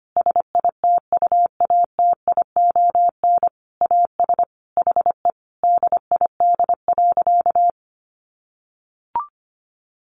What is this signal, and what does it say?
Morse code, heard as a keyed tone, about 25 wpm, 700 Hz. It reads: SITUATION AS HE DID.
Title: Sentences from Top 1200 Words 25wpm